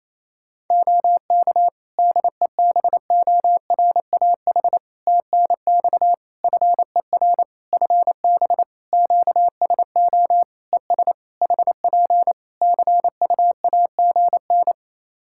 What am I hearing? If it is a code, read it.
Morse code, heard as a keyed tone, about 28 wpm, 700 Hz. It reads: OK DEBORAH TNX FER FB QSO ES HP CUAGN
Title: Practice QSO - Exchange 4 - Ending - 28wpm